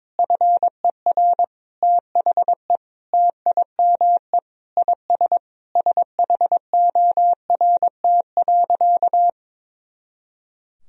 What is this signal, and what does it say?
Morse code, heard as a keyed tone, about 22 wpm, 700 Hz. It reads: FER THE TIME IS SHORT.
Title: Sentences from Top 500 Words and Abbreviations 22wpm